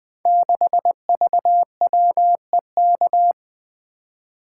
Morse code, heard as a keyed tone, about 20 wpm, 700 Hz. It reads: 6VWEK
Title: Code Groups - 5 Letter:Numbers - 20wpm